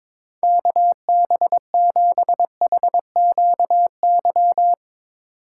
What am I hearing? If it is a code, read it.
Morse code, heard as a keyed tone, about 22 wpm, 700 Hz. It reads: KB7HQY
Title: USA Call Signs - 6 Characters - 2x Character Spacing - 22wpm